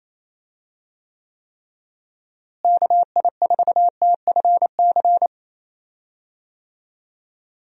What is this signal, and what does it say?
Morse code, heard as a keyed tone, about 27 wpm, 700 Hz. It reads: KI4TFC
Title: USA Call Signs - 6 Characters - Repeated 3x - 28wpm